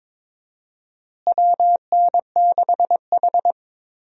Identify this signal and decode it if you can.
Morse code, heard as a keyed tone, about 22 wpm, 700 Hz. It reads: WN6H
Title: USA Call Signs - 4 Characters - Repeated 2x - 22wpm